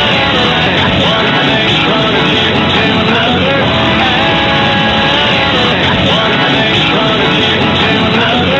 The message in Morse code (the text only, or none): none